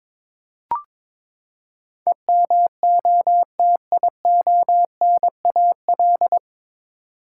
Morse code, heard as a keyed tone, about 22 wpm, 700 Hz. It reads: EMOTIONAL
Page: Morse Code Ninja practice 22wpm